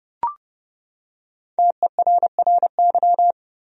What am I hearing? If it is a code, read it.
Morse code, heard as a keyed tone, about 31 wpm, 700 Hz. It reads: TERRY
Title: USA Common Names - Max 5 Letters - 30wpm